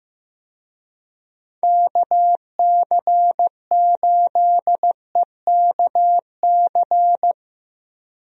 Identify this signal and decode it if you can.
Morse code, heard as a keyed tone, about 15 wpm, 700 Hz. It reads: KC8EKC
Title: USA Call Signs - 6 Characters - Repeated 2x - 15wpm